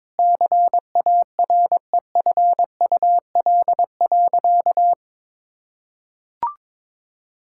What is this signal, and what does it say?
Morse code, heard as a keyed tone, about 22 wpm, 700 Hz. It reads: CAREFUL.
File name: practice / 22wpm / Sentences with 3-Word Phrases - 22wpm